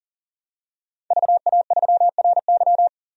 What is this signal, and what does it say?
Morse code, heard as a keyed tone, about 40 wpm, 700 Hz. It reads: VA3RY
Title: DXCC-100 Call Signs - 5 Characters - 40wpm